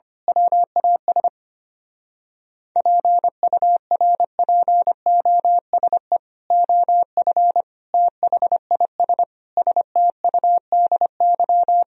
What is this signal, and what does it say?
Morse code, heard as a keyed tone, about 24 wpm, 700 Hz. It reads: WAS PURPOSE OF THIS STUDY
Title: Speed-Racing - Sets of 5 Words 25wpm